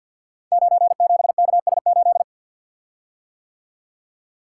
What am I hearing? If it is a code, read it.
Morse code, heard as a keyed tone, about 52 wpm, 700 Hz. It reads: 97KS8
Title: Code Groups - 5 Letter:Numbers - 50wpm